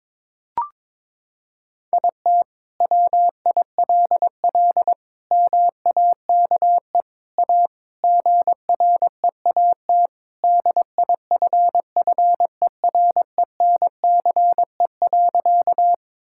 Morse code, heard as a keyed tone, about 22 wpm, 700 Hz. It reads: IT WILL MAKE A GREAT DIFFERENCE.
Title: Sentences from Top 1000 Words - Review Part 5 - 22wpm